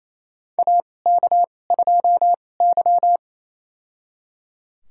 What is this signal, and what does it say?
Morse code, heard as a keyed tone, about 28 wpm, spaced as for 20 wpm, 700 Hz. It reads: AK2Y
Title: USA Call Signs - 4 Characters - 2x Character Spacing - 28wpm